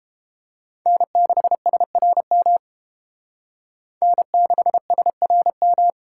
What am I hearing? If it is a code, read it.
Morse code, heard as a keyed tone, about 29 wpm, 700 Hz. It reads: N6SRM N6SRM